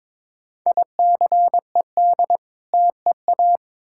Morse code, heard as a keyed tone, about 22 wpm, 700 Hz. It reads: ICED TEA
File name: Difficult Words with Hints - 22wpm